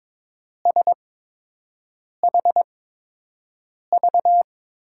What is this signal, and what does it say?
Morse code, heard as a keyed tone, about 22 wpm, spaced as for 6 wpm, 700 Hz. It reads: SHV
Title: USA Call Signs - 5 Characters - 8x Character Spacing - 22wpm